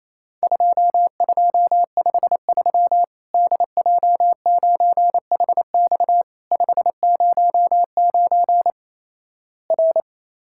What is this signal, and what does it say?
Morse code, heard as a keyed tone, about 28 wpm, 700 Hz. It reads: 2253 DJ9HX 509 R
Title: Worked All Europe DX Contest - QTC Exchange - 28wpm